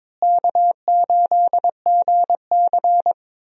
Morse code, heard as a keyed tone, about 22 wpm, 700 Hz. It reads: K8GC
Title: USA Call Signs - 4 Characters - Repeated 3x - 22wpm